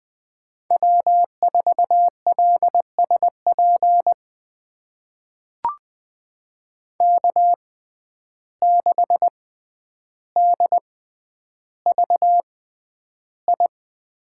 Morse code, heard as a keyed tone, about 21 wpm, spaced as for 10 wpm, 700 Hz. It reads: W4LSP K6DVI
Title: USA Call Signs - 5 Characters - 6x Character Spacing - 20wpm